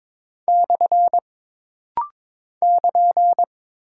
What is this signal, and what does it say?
Morse code, heard as a keyed tone, about 22 wpm, 700 Hz. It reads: / <KN>
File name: practice / 22wpm / Single Punctuation-Prosign - Rapid-Fire - 22wpm